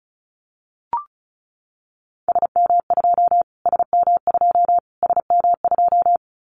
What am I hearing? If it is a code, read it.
Morse code, heard as a keyed tone, about 34 wpm, 700 Hz. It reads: SM2 SM2 SM2
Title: Callsign Prefixes for Top 25 DXCC - Repeated 3x - 35wpm